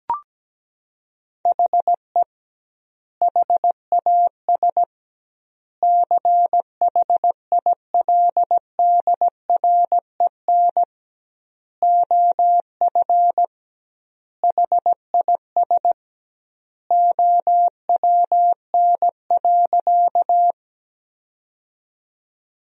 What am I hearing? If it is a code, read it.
Morse code, heard as a keyed tone, about 17 wpm, 700 Hz. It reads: HE HAS CHILDREN OF HIS OWN.
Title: 2x Word-Spacing - Sentences from Top 300 Words - 17wpm